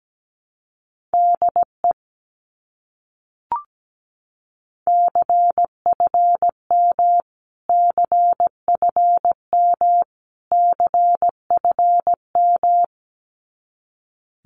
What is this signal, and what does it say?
Morse code, heard as a keyed tone, about 17 wpm, 700 Hz. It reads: DE CFM CFM CFM
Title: Abbreviations Repeated 3x - 17wpm